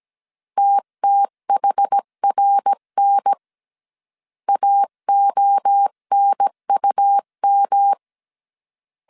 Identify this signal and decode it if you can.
Morse code, heard as a keyed tone, about 17 wpm, 800 Hz. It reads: TTHRN AONUM
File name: CW / DET / 1 3 1500C S A 16wpm 2141